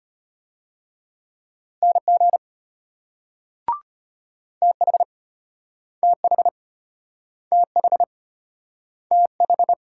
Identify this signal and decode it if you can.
Morse code, heard as a keyed tone, about 34 wpm, 700 Hz. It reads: NG TH TH TH TH